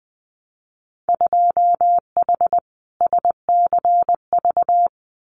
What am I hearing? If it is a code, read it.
Morse code, heard as a keyed tone, about 20 wpm, 700 Hz. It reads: 2H SCV
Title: Winter Field Day Exchanges Encore 20wpm